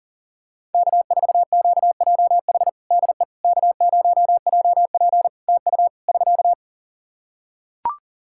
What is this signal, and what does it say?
Morse code, heard as a keyed tone, about 38 wpm, 700 Hz. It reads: K4QJH DE K0JP TU <SK>
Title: Practice QSO - Exchange 4 - Ending - 40wpm